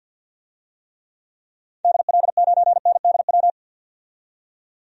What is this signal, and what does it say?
Morse code, heard as a keyed tone, about 50 wpm, 700 Hz. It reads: DL9NDW
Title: DXCC-100 Call Signs - 6 Characters - 50wpm